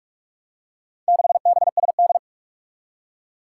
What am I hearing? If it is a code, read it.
Morse code, heard as a keyed tone, about 42 wpm, 700 Hz. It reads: 6BSD
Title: Four Challenging Letter-Number Combinations 45wpm